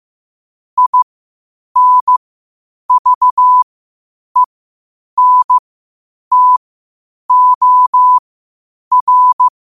Morse code, heard as a keyed tone, about 15 wpm, spaced as for 8 wpm, 1000 Hz. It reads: INVENTOR
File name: qotdfiles / 8wpm